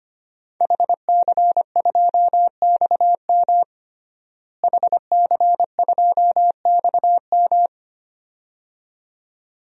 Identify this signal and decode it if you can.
Morse code, heard as a keyed tone, about 25 wpm, 700 Hz. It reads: HC2XM HC2XM
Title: DXCC-100 Call Signs - 5 Characters - Repeated 2x - 25wpm